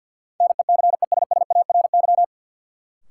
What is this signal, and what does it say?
Morse code, heard as a keyed tone, about 50 wpm, 700 Hz. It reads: NECESSARY
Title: Sentences from Top 800 Words - New Word Drill - 50wpm